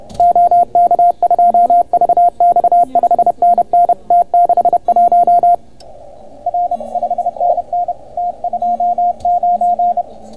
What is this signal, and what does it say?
Morse code, heard as a keyed tone, about 31 wpm, 690 Hz, with radio noise. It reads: OK2VX 5NN T61 R 5NN T29